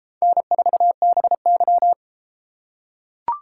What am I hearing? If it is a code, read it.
Morse code, heard as a keyed tone, about 30 wpm, 700 Hz. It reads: N4BY